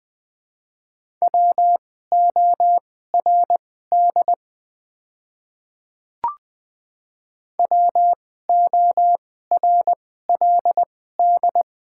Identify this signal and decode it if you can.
Morse code, heard as a keyed tone, about 20 wpm, spaced as for 14 wpm, 700 Hz. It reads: WORD WORLD
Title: 2x Character Spacing - Top 200 Words - 20wpm